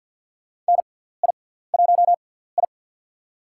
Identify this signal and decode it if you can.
Morse code, heard as a keyed tone, about 48 wpm, spaced as for 17 wpm, 690 Hz. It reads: NI1I
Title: USA Call Signs - 4 Characters - 6x Character Spacing - 50wpm